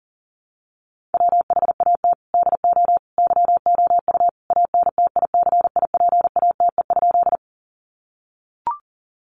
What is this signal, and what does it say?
Morse code, heard as a keyed tone, about 40 wpm, 700 Hz. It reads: WHAT DO YOU ANTICIPATE?
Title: Sentences from Top 1500 Words 40wpm